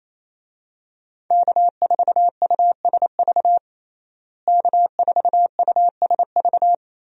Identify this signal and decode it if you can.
Morse code, heard as a keyed tone, about 28 wpm, 700 Hz. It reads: K4USV K4USV
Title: USA Call Signs - 5 Characters - Repeated 2x - 28wpm